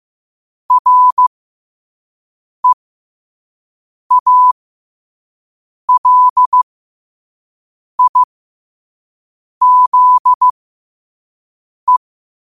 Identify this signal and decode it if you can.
Morse code, heard as a keyed tone, about 15 wpm, spaced as for 5 wpm, 1000 Hz. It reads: REALIZE
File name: qotdfiles / 5wpm